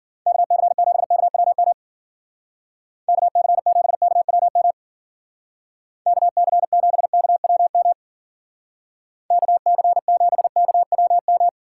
Morse code, heard as a keyed tone, about 50 wpm, 700 Hz. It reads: KC7KWM KC7KWM KC7KWM KC7KWM
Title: Speed Racing - USA 2x3 Call Signs 40wpm